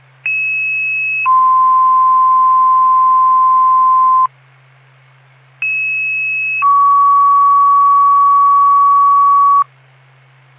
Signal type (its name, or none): two-tone page